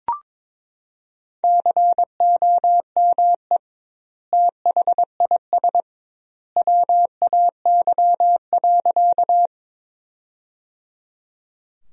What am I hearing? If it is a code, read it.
Morse code, heard as a keyed tone, about 22 wpm, 700 Hz. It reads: COME THIS WAY.
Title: 2x Word-Spacing - Sentences from Top 100 Words - 22wpm